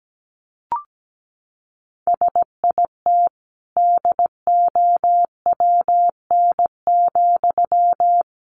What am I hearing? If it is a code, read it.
Morse code, heard as a keyed tone, about 17 wpm, 700 Hz. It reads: SIT DOWN,
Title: Sentences from Top 500 Words 17wpm